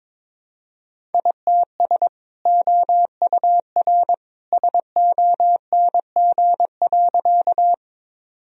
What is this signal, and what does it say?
Morse code, heard as a keyed tone, about 22 wpm, 700 Hz. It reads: ITS OUR SONG.
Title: Sentences from Top 1000 Words - Review Part 2 - 22wpm